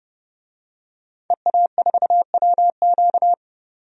Morse code, heard as a keyed tone, about 30 wpm, 700 Hz. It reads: EA4WQ
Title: DXCC-100 Call Signs - 5 Characters - 30wpm